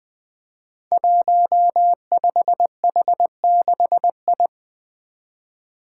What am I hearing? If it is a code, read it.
Morse code, heard as a keyed tone, about 20 wpm, 700 Hz. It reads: 15H6I
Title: Five Challenging Letter-Number Combinations 20wpm